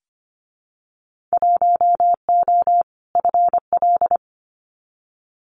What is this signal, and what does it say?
Morse code, heard as a keyed tone, about 25 wpm, 700 Hz. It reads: 1O FL
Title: Winter Field Day Exchanges 25wpm